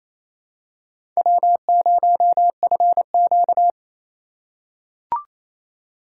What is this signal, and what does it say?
Morse code, heard as a keyed tone, about 28 wpm, 700 Hz. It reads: W0FQ
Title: USA Call Signs - 2x Character Spacing - 28wpm